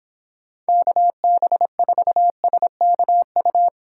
Morse code, heard as a keyed tone, about 24 wpm, 700 Hz. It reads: KB4SKU